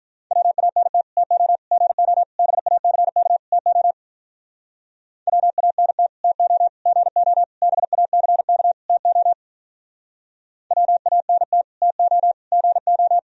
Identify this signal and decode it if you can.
Morse code, heard as a keyed tone, about 45 wpm, 700 Hz. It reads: WANT TO GO BACK TO WANT TO GO BACK TO WANT TO GO